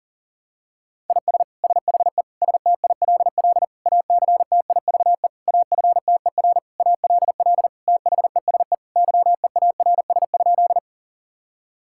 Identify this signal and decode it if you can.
Morse code, heard as a keyed tone, about 40 wpm, 700 Hz. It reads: IS SHE STILL ACTIVE AFTER ALL THESE YEARS?
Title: Sentences from Top 1000 Words - Review Part 6 - 40wpm